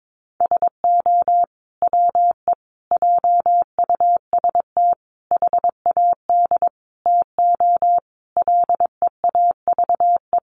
Morse code, heard as a keyed tone, about 22 wpm, 700 Hz. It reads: SO WE JUST HAD TO LEAVE
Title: Sentences from Top 1500 Words 22wpm